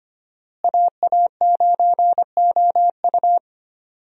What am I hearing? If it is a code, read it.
Morse code, heard as a keyed tone, about 25 wpm, 700 Hz. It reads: AA9OU
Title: USA Call Signs - 5 Characters - Repeated 3x - 25wpm